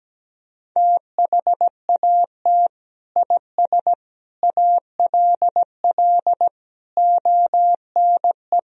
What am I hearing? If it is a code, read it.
Morse code, heard as a keyed tone, about 17 wpm, 700 Hz. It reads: THAT IS ALL ONE
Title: Sentences from Top 200 Words and Abbreviations 17wpm